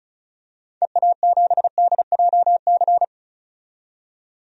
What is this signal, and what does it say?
Morse code, heard as a keyed tone, about 36 wpm, 700 Hz. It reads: EA7DJC
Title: DXCC-100 Call Signs - 6 Characters - 35wpm